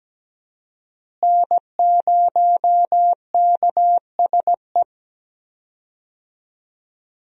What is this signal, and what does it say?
Morse code, heard as a keyed tone, about 17 wpm, 700 Hz. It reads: N0KSE
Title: USA Call Signs - 5 Characters - Repeated 2x - 17wpm